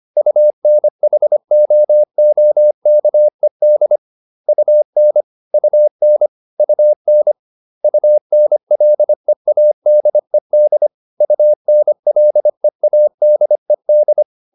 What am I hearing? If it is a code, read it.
Morse code, heard as a keyed tone, about 25 wpm, 600 Hz. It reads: UNHOOKED UN UN UN UNLEADED UNLEADED